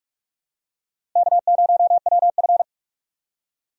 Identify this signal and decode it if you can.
Morse code, heard as a keyed tone, about 44 wpm, 700 Hz. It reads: K0WF
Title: USA Call Signs - 4 Characters - Repeated 3x - 45wpm